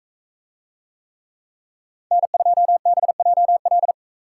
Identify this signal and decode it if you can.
Morse code, heard as a keyed tone, about 40 wpm, 700 Hz. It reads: N2BJL